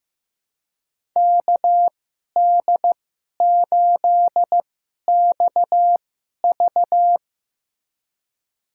Morse code, heard as a keyed tone, about 15 wpm, spaced as for 11 wpm, 700 Hz. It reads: KD8XV
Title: USA Call Signs - 2x Character Spacing - 15wpm